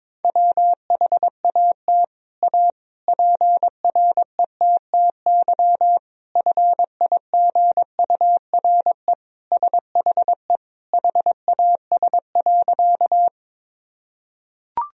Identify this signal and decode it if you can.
Morse code, heard as a keyed tone, about 22 wpm, 700 Hz. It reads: WHAT A PRETTY FIGURE SHE HAS.
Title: Sentences from Top 1000 Words - Review Part 2 - 22wpm